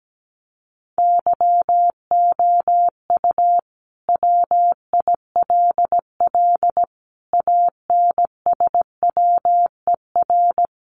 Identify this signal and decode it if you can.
Morse code, heard as a keyed tone, about 17 wpm, 700 Hz. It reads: YOU WILL ANSWER
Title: Sentences from Top 300 Words 17wpm